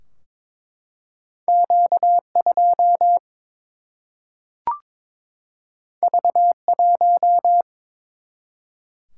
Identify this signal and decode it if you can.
Morse code, heard as a keyed tone, about 22 wpm, 700 Hz. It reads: Q2 V1
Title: Single-Letter Single-Number 22wpm